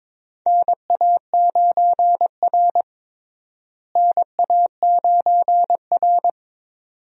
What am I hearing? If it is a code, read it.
Morse code, heard as a keyed tone, about 22 wpm, 700 Hz. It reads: NA9R NA9R
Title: USA Call Signs - 4 Characters - Repeated 2x - 22wpm